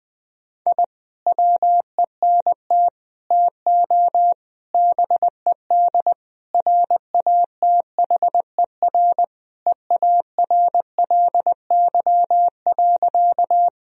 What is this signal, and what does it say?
Morse code, heard as a keyed tone, about 20 wpm, 700 Hz. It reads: I WENT TO BED RATHER EARLY.
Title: Sentences from Top 1000 Words - Review Part 4 - 20wpm